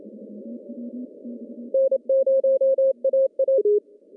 Morse code, heard as a keyed tone, about 28 wpm, 540 Hz. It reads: N0AA
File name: CW / cqww-cw-k5tr